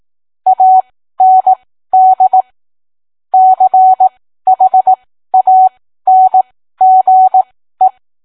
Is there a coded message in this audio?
Yes: Morse code from 0.5 to 7.9 s, heard as a keyed tone, about 18 wpm, spaced as for 13 wpm, 750 Hz. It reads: AND CHANGE